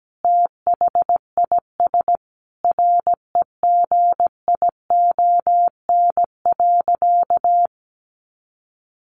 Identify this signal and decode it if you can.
Morse code, heard as a keyed tone, about 17 wpm, 700 Hz. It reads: THIS REGION.
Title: Sentences from Top 1500 Words 17wpm